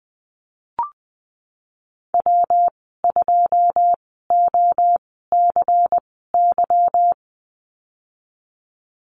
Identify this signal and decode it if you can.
Morse code, heard as a keyed tone, about 20 wpm, spaced as for 15 wpm, 700 Hz. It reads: W2OCY
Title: 147 Lesson 24g - Letter y - Call Signs 20wpm